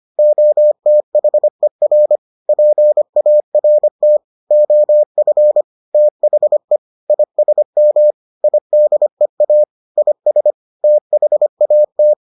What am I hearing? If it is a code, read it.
Morse code, heard as a keyed tone, about 25 wpm, 600 Hz. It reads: OTHER PART OF THE ISM IDEA IS THAT